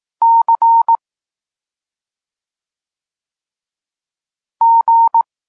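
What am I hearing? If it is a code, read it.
Morse code, heard as a keyed tone, about 18 wpm, 920 Hz. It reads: C G
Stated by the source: Polish Morse lesson teaching the letter C, from SP9MOA Niepołomicki Klub Krótkofalowców.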